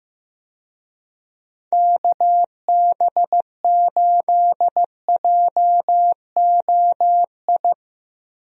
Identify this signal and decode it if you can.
Morse code, heard as a keyed tone, about 15 wpm, 700 Hz. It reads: KB8JOI
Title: USA Call Signs - 6 Characters - 15wpm